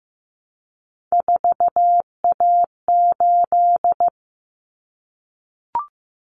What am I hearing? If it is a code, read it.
Morse code, heard as a keyed tone, about 15 wpm, 700 Hz. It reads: 4A8
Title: Common Cut Numbers - 3-Digit 15wpm